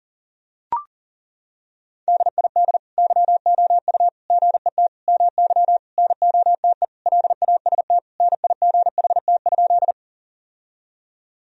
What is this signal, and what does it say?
Morse code, heard as a keyed tone, about 40 wpm, 700 Hz. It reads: DID YOU GET MY NOTE LAST NIGHT?
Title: Sentences from Top 1000 Words - Review Part 2 - 40wpm